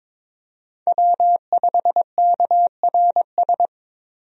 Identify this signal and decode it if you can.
Morse code, heard as a keyed tone, about 22 wpm, 700 Hz. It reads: W5KRS
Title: USA Call Signs - 5 Characters - 22wpm